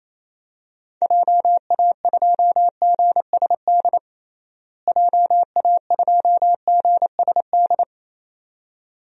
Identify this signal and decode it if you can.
Morse code, heard as a keyed tone, about 28 wpm, 700 Hz. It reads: JA2GSD JA2GSD
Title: DXCC-100 Call Signs - 6 Characters - Repeated 2x - 28wpm